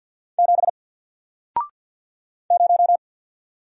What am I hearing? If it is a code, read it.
Morse code, heard as a keyed tone, about 48 wpm, 700 Hz. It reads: 7 0
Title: Single Character (Letter:Number:Punctuation:Prosign) Alt 50wpm